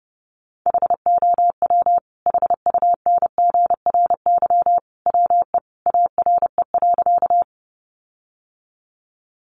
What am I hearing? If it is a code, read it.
Morse code, heard as a keyed tone, about 30 wpm, 700 Hz. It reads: HOW HUNGRY WE ARE.
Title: Sentences from Top 1400 Words 30wpm